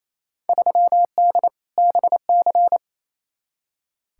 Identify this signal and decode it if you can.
Morse code, heard as a keyed tone, about 28 wpm, 700 Hz. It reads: 3D BC
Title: Field Day Exchanges Encore - 28wpm